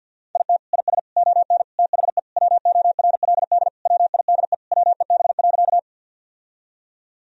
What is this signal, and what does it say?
Morse code, heard as a keyed tone, about 50 wpm, 700 Hz. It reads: IT IS ON THE WORLD WIDE WEB.
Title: Sentences from Top 1000 Words - Review Part 4 - 50wpm